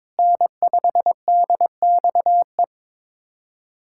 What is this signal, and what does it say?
Morse code, heard as a keyed tone, about 22 wpm, 700 Hz. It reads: N5DXE
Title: USA Callsign DE Callsign - 5 Characters - 22wpm